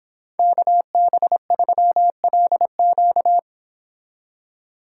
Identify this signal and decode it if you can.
Morse code, heard as a keyed tone, about 25 wpm, 700 Hz. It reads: KB3LQ
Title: Speed Racing - USA 2x2 Call Signs 22wpm